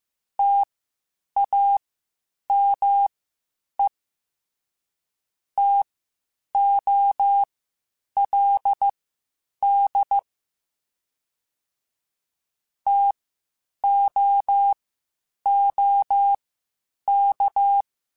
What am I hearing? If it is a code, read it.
Morse code, heard as a keyed tone, about 15 wpm, spaced as for 8 wpm, 800 Hz. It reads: TAME TOLD TOOK